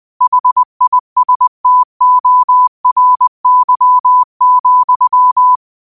Morse code, heard as a keyed tone, about 20 wpm, 1000 Hz. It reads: HISTORY,